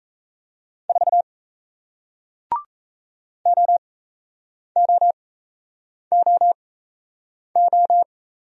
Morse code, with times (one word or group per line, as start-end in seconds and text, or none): 0.9-1.2 4
3.5-8.0 OOOO